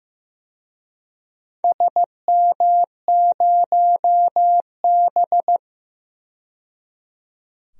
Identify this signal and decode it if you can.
Morse code, heard as a keyed tone, about 15 wpm, 700 Hz. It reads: SM0B
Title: DXCC-100 Call Signs - 4 Characters - 15wpm